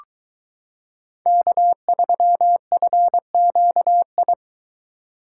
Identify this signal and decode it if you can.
Morse code, heard as a keyed tone, about 22 wpm, 700 Hz. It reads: K3FQI